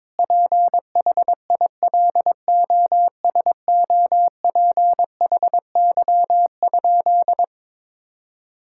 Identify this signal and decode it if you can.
Morse code, heard as a keyed tone, about 22 wpm, 700 Hz. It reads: PHILOSOPHY?
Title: Sentences from Top 1000 Words - Review Part 6 - 22wpm